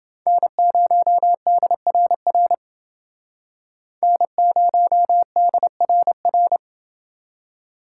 Morse code, heard as a keyed tone, about 30 wpm, 700 Hz. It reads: N0DRR N0DRR